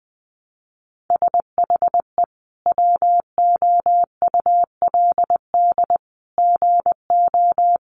Morse code, heard as a keyed tone, about 20 wpm, 700 Hz. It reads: SHE WOULD GO
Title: Sentences from Top 1300 Words 20wpm